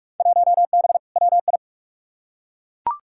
Morse code, heard as a keyed tone, about 42 wpm, 700 Hz. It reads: 1B WI
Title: Field Day Exchanges Encore - 45wpm